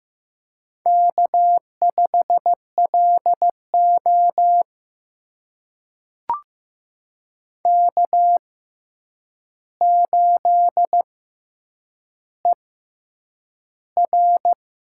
Morse code, heard as a keyed tone, about 15 wpm, spaced as for 8 wpm, 700 Hz. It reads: K5LO K8ER
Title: USA Call Signs - 4 Characters - 6x Character Spacing - 15wpm